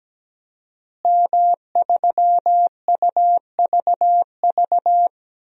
Morse code, heard as a keyed tone, about 17 wpm, 700 Hz. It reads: M3UVV
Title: DXCC-100 Call Signs - 5 Characters - Repeated 3x - 17wpm